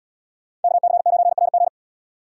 Morse code, heard as a keyed tone, about 70 wpm, 700 Hz. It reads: KC9RG